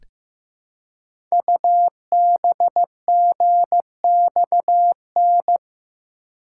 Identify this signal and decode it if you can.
Morse code, heard as a keyed tone, about 15 wpm, 700 Hz. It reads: UBGXN
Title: Code Groups - 5 Letters - 15wpm